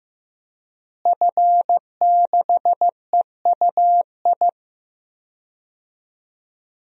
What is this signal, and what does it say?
Morse code, heard as a keyed tone, about 15 wpm, 700 Hz. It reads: F6EUI
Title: DXCC-100 Call Signs - 5 Characters - Repeated 2x - 15wpm